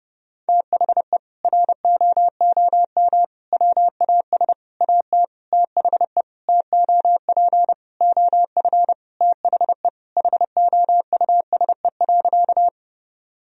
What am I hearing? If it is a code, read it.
Morse code, heard as a keyed tone, about 30 wpm, 700 Hz. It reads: THE ROOM WAS AT THE TOP OF THE HOUSE.
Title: Sentences from Top 1000 Words - Review Part 2 - 30wpm